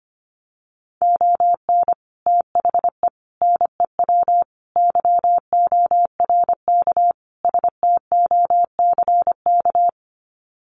Morse code, heard as a keyed tone, about 25 wpm, 700 Hz. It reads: ON THE NEW YORK STOCK
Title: Sets of 5 Words Encore 25wpm